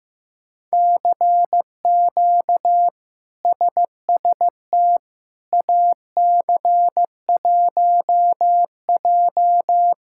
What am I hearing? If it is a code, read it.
Morse code, heard as a keyed tone, about 15 wpm, 700 Hz. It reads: CQ SST AC1J
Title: K1USN SST QSO Practice - 15wpm